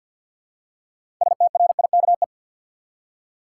Morse code, heard as a keyed tone, about 53 wpm, 700 Hz. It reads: STRIKE
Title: Six Letter Words - 50wpm